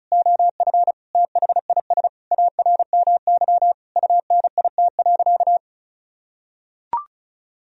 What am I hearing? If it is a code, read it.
Morse code, heard as a keyed tone, about 36 wpm, 700 Hz. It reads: OF THIS ARMY UNIT.
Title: Sentences from Top 1000 Words - Review Part 2 - 35wpm